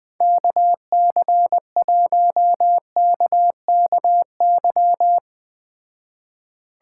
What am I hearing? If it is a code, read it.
Morse code, heard as a keyed tone, about 21 wpm, 700 Hz. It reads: KC1KKY